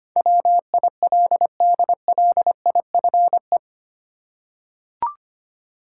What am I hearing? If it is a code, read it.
Morse code, heard as a keyed tone, about 25 wpm, 700 Hz. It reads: WILDLIFE
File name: Eight Letter Words - 25wpm